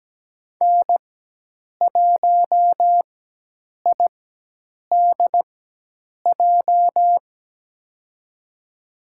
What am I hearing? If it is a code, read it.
Morse code, heard as a keyed tone, about 17 wpm, spaced as for 8 wpm, 700 Hz. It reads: N1IDJ
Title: USA Call Signs - 5 Characters - 4x Character Spacing - 17wpm